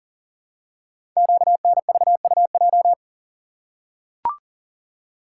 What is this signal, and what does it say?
Morse code, heard as a keyed tone, about 40 wpm, 690 Hz. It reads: QNVUJ
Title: Code Groups - 5 Letters - 40wpm